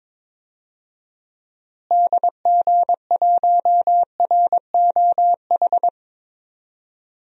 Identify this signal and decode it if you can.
Morse code, heard as a keyed tone, about 22 wpm, 700 Hz. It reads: DG1ROH
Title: DXCC-100 Call Signs - 6 Characters - Repeated 2x - 22wpm